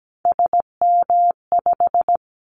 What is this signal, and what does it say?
Morse code, heard as a keyed tone, about 17 wpm, 700 Hz. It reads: SM5
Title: Callsign Prefixes for Top 25 DXCC - 17wpm